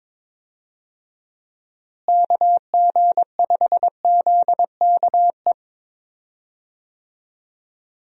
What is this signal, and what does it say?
Morse code, heard as a keyed tone, about 22 wpm, 700 Hz. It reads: KG5ZKE
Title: USA Call Signs - 6 Characters - Repeated 2x - 22wpm